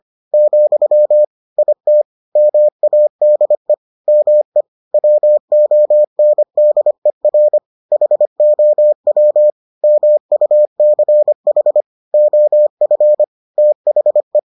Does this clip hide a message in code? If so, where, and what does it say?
Yes: Morse code from 0.3 to 14.4 s, heard as a keyed tone, about 25 wpm, 600 Hz. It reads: , IT MADE ME WONDER HOW MUCH OF THE